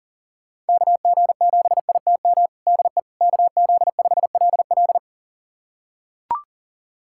Morse code, heard as a keyed tone, about 40 wpm, 700 Hz. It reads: KG7ITM DE KZ5LL
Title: USA Callsign DE Callsign 40wpm